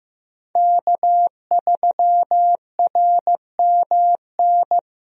Morse code, heard as a keyed tone, about 15 wpm, 700 Hz. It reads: K3RMN
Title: USA Call Signs - 5 Characters - Repeated 3x - 15wpm